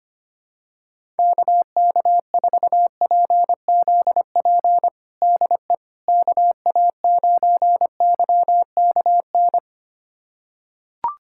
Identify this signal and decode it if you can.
Morse code, heard as a keyed tone, about 25 wpm, 700 Hz. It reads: KK4PZP DE KA9YKN
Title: USA Callsign DE Callsign - 6 Characters - 25wpm